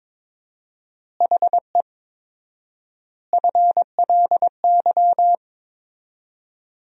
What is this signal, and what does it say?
Morse code, heard as a keyed tone, about 23 wpm, 700 Hz. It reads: HE FLY
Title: 4x Word-Spacing - Sentences from Top 500 Words - 22wpm